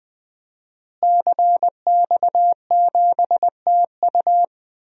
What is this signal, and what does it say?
Morse code, heard as a keyed tone, about 21 wpm, 700 Hz. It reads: CX7TU
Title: DXCC-100 Call Signs - 5 Characters - Repeated 3x - 20wpm